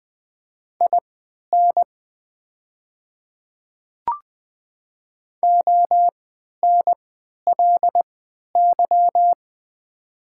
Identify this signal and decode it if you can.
Morse code, heard as a keyed tone, about 20 wpm, spaced as for 11 wpm, 690 Hz. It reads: IN ONLY
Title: 3x Character Spacing - Top 200 Words - 20wpm